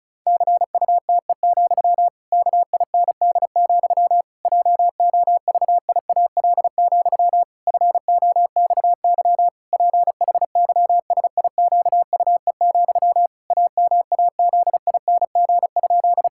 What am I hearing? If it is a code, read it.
Morse code, heard as a keyed tone, about 34 wpm, 700 Hz. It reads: CUTE, KIND, JOVIAL, FOXY PHYSIQUE, AMAZING?
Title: Pangram Challenge v2 - 35wpm